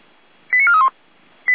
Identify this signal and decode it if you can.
Five-tone selective call (ZVEI), heard as quick runs of five tones, tones of about 70 ms. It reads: 89431